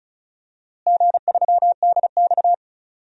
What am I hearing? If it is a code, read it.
Morse code, heard as a keyed tone, about 36 wpm, 690 Hz. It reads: G3DX
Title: DXCC-100 Call Signs - 4 Characters - Repeated 2x - 35wpm